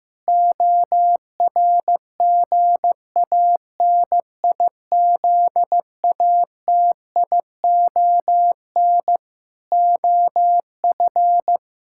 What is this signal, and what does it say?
Morse code, heard as a keyed tone, about 15 wpm, 700 Hz. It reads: ORGANIZATION OF
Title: Sentences from Top 1000 Words - Review Part 6 - 15wpm